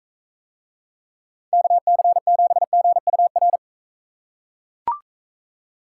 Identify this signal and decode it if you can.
Morse code, heard as a keyed tone, about 43 wpm, 700 Hz. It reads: KC7GUR